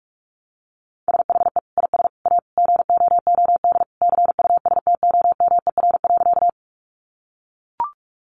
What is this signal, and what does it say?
Morse code, heard as a keyed tone, about 44 wpm, 700 Hz. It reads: SHE IS A GOOD CUSTOMER.
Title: Sentences from Top 1100 Words 45wpm